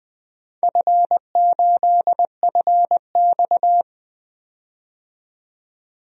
Morse code, heard as a keyed tone, about 20 wpm, 700 Hz. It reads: F8FX
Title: DXCC-100 Call Signs - 4 Characters - 20wpm